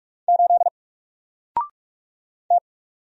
Morse code, heard as a keyed tone, about 44 wpm, 700 Hz. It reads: 8 T
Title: Single Character (Letter:Number:Punctuation:Prosign) 45wpm